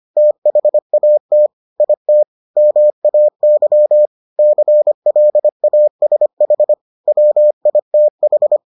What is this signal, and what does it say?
Morse code, heard as a keyed tone, about 25 wpm, 600 Hz. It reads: THAT IT MAY CLASH WITH